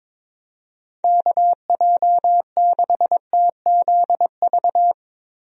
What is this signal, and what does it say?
Morse code, heard as a keyed tone, about 22 wpm, 700 Hz. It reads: KJ6TZV
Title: USA Call Signs - 6 Characters - Repeated 2x - 22wpm